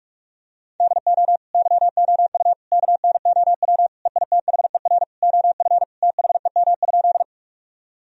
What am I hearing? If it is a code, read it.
Morse code, heard as a keyed tone, about 46 wpm, 700 Hz. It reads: DO YOU KNOW EITHER OF THEM?
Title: Sentences from Top 1000 Words - Review Part 4 - 45wpm